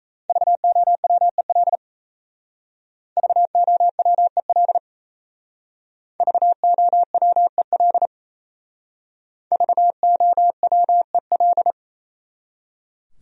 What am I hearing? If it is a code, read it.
Morse code, heard as a keyed tone, about 35 wpm, 700 Hz. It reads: VOWEL VOWEL VOWEL VOWEL
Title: Speed-Racing Top 500 Words 28wpm